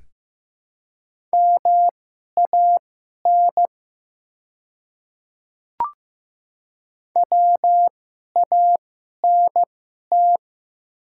Morse code, heard as a keyed tone, about 15 wpm, spaced as for 11 wpm, 700 Hz. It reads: MAN WANT
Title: 2x Character Spacing - Top 200 Words - 15wpm